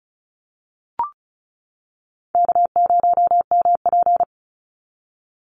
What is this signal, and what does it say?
Morse code, heard as a keyed tone, about 34 wpm, 700 Hz. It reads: K0MP
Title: NA SOTA Chaser Callsigns 35wpm